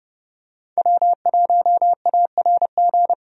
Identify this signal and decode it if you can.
Morse code, heard as a keyed tone, about 30 wpm, 700 Hz. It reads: W1ARG
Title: USA Call Signs - 5 Characters - Repeated 2x - 30wpm